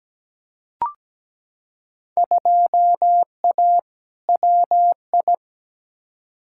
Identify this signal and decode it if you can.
Morse code, heard as a keyed tone, about 17 wpm, 700 Hz. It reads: 2A WI
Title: Field Day Exchanges - 17wpm